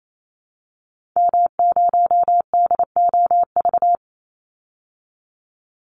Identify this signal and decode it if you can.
Morse code, heard as a keyed tone, about 28 wpm, 700 Hz. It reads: M0DOV
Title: Call Signs Repeated 2x - 28wpm